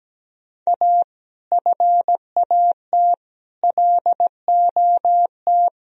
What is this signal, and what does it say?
Morse code, heard as a keyed tone, about 17 wpm, 700 Hz. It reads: A FAT LOT
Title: Sentences from Top 1000 Words - Review Part 4 - 17wpm